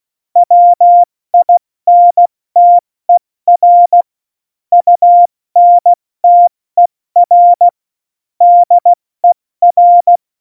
Morse code, heard as a keyed tone, about 16 wpm, 700 Hz. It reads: WINTER UNTER DER
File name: Morsekurs / Witze / Witz 07. Januar 2020, 70 F80